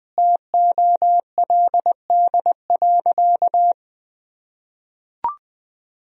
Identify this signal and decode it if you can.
Morse code, heard as a keyed tone, about 21 wpm, 700 Hz. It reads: TOLD.
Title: Sentences from Top 400 Words and Abbreviations 20wpm